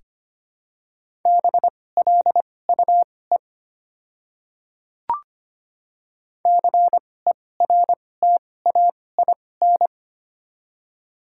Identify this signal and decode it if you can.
Morse code, heard as a keyed tone, about 25 wpm, spaced as for 18 wpm, 700 Hz. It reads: BLUE CERTAIN
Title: 2x Character Spacing - Top 500 Words - 25wpm